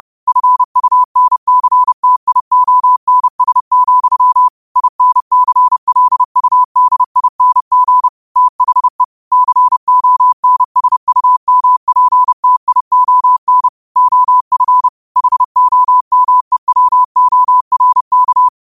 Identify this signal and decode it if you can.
Morse code, heard as a keyed tone, about 30 wpm, 1000 Hz. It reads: FUNCTIONS, INCLUDING THE CONSUMPTION OF HOMEWORK
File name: 30wpm